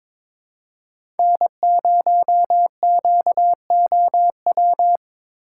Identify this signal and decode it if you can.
Morse code, heard as a keyed tone, about 22 wpm, 700 Hz. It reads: N0QOW
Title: USA Call Signs - 6x Character Spacing - 22wpm